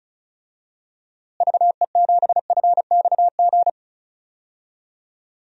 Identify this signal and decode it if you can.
Morse code, heard as a keyed tone, about 36 wpm, 700 Hz. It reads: VE7FXG
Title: DXCC-100 Call Signs - 6 Characters - Repeated 2x - 35wpm